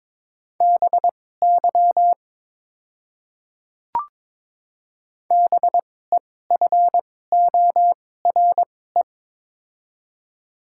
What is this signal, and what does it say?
Morse code, heard as a keyed tone, about 22 wpm, spaced as for 16 wpm, 700 Hz. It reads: BY BEFORE
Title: 2x Character Spacing - Top 300 Words - 22wpm